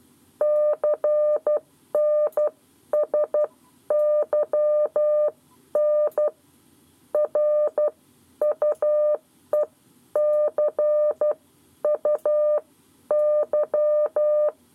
Morse code, heard as a keyed tone, about 12 wpm, spaced as for 10 wpm, 600 Hz. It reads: CNSYN R UEC U Y
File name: 7 Lezione cw 1